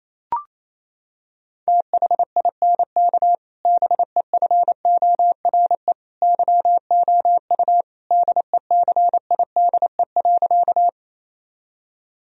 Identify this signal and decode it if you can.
Morse code, heard as a keyed tone, about 28 wpm, 700 Hz. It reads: THINK BEFORE YOU DECIDE.
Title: Sentences from Top 1000 Words - Review Part 3 - 28wpm